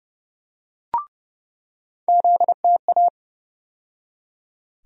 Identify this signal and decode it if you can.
Morse code, heard as a keyed tone, about 30 wpm, 700 Hz. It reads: ZTA
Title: USA Call Sign Postfixes - 3 Letters 30wpm